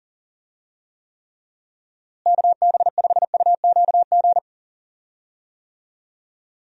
Morse code, heard as a keyed tone, about 40 wpm, 700 Hz. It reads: KB5UQG
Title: USA Call Signs - 6 Characters - Repeated 3x - 40wpm